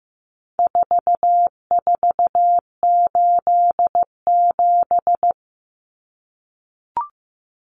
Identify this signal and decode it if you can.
Morse code, heard as a keyed tone, about 15 wpm, 700 Hz. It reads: 4487
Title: Numbers - 4-Digit 15wpm